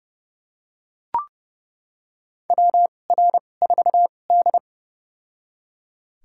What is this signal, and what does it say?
Morse code, heard as a keyed tone, about 32 wpm, spaced as for 22 wpm, 700 Hz. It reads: WR4D
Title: USA Call Signs - 4 Characters - 2x Character Spacing - 30wpm